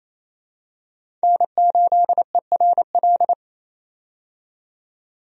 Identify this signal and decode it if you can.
Morse code, heard as a keyed tone, about 27 wpm, 700 Hz. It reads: N8ERL
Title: USA Call Signs - 5 Characters - 28wpm